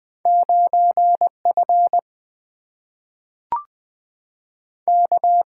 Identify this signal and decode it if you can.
Morse code, heard as a keyed tone, about 20 wpm, 700 Hz. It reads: 9F K
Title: Single-Letter Single-Number 20wpm